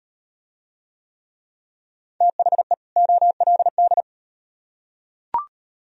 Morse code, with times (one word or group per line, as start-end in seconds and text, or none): 2.2-2.7 THE
3.0-4.0 OLD